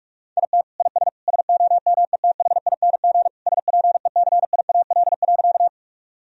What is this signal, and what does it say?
Morse code, heard as a keyed tone, about 46 wpm, 700 Hz. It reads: IT IS SOMETHING SPECIAL.
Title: Sentences from Top 1000 Words - Review Part 3 - 45wpm